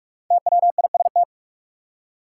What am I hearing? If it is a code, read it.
Morse code, heard as a keyed tone, about 46 wpm, 700 Hz. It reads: TWIST
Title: Five Letter Words - 45wpm